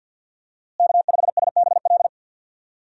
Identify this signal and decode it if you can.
Morse code, heard as a keyed tone, about 51 wpm, 700 Hz. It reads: K5SBL
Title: USA Call Signs - 5 Characters - Repeated 2x - 50wpm